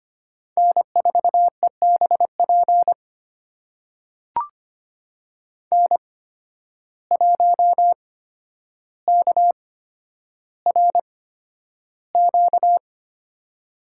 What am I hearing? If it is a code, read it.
Morse code, heard as a keyed tone, about 25 wpm, spaced as for 11 wpm, 700 Hz. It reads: N4EBP N1KRQ
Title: USA Call Signs - 5 Characters - 8x Character Spacing - 25wpm